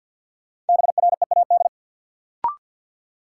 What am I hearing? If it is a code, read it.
Morse code, heard as a keyed tone, about 53 wpm, 700 Hz. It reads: BREAD
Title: Sentences from Top 1600 Words - New Word Drill - 50wpm